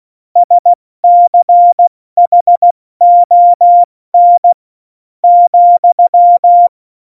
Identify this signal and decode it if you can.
Morse code, heard as a keyed tone, about 16 wpm, spaced as for 13 wpm, 700 Hz. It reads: SCHON,
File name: Morsekurs / Witze / Witz 28. Januar 2020, 70 F80